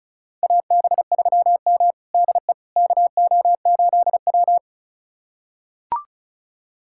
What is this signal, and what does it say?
Morse code, heard as a keyed tone, about 36 wpm, 700 Hz. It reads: AB3M DE KO8W
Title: USA Callsign DE Callsign - 4 Characters - 35wpm